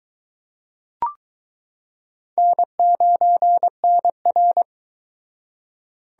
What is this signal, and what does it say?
Morse code, heard as a keyed tone, about 22 wpm, 700 Hz. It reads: N9NR